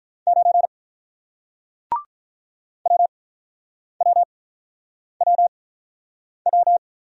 Morse code, heard as a keyed tone, about 48 wpm, spaced as for 9 wpm, 700 Hz. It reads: 9 WWWW